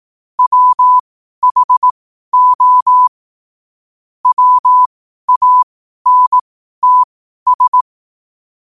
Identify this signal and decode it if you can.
Morse code, heard as a keyed tone, about 18 wpm, spaced as for 12 wpm, 1000 Hz. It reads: WHO WANTS
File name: qotdfiles / 12wpm